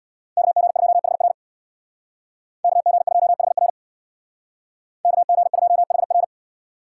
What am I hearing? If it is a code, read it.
Morse code, heard as a keyed tone, about 64 wpm, 700 Hz. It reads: KG2HR KG2HR KG2HR